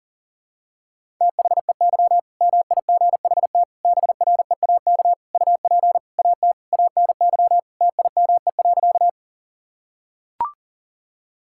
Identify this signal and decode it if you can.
Morse code, heard as a keyed tone, about 40 wpm, 700 Hz. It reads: THEY MIGHT BREAK UP AT ANY TIME.
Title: Sentences from Top 1000 Words - Review Part 6 - 40wpm